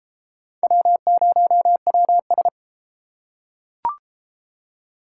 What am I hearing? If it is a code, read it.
Morse code, heard as a keyed tone, about 29 wpm, 700 Hz. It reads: W0WS